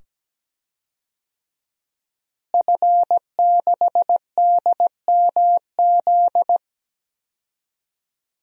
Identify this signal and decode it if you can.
Morse code, heard as a keyed tone, about 17 wpm, 700 Hz. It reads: F6DMZ
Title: DXCC-100 Call Signs - 5 Characters - Repeated 2x - 17wpm